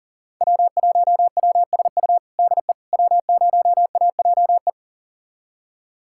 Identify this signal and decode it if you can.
Morse code, heard as a keyed tone, about 40 wpm, 700 Hz. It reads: W1WSU DE W0AJE
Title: USA Callsign DE Callsign - 5 Characters - 40wpm